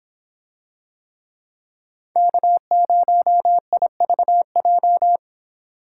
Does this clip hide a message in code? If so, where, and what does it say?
Yes: Morse code from 2.2 to 5.2 s, heard as a keyed tone, about 25 wpm, 700 Hz. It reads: K0IVJ